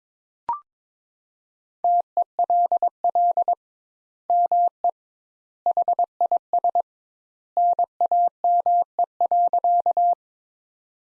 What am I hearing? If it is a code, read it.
Morse code, heard as a keyed tone, about 21 wpm, 700 Hz. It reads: TELL ME HIS NAME.
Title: Sentences Top 200 Words - 22wpm